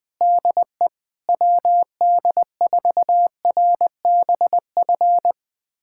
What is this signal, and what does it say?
Morse code, heard as a keyed tone, about 20 wpm, 700 Hz. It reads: DE WD4RBF
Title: USA Callsign DE Callsign - 6 Characters - 20wpm